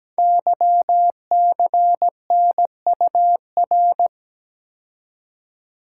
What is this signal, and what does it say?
Morse code, heard as a keyed tone, about 17 wpm, 700 Hz. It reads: YCNUR
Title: Code Groups - 5 Letter:Numbers - 17wpm